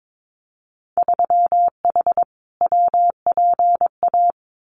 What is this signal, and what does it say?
Morse code, heard as a keyed tone, about 22 wpm, 700 Hz. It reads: 3H WPA
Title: Winter Field Day Exchanges 22wpm